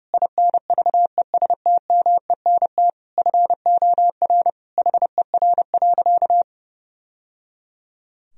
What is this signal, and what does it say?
Morse code, heard as a keyed tone, about 28 wpm, 700 Hz. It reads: INVESTMENT FOR HER.
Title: Sentences from Top 1000 Words - Review Part 4 - 30wpm